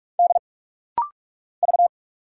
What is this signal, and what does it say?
Morse code, heard as a keyed tone, about 46 wpm, 700 Hz. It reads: D V
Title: Single Letter-Number Alt - Rapid-Fire 45wpm